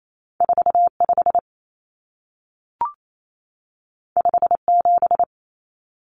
Morse code, heard as a keyed tone, about 28 wpm, 700 Hz. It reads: 45 57
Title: Numbers - 2-Digit 28wpm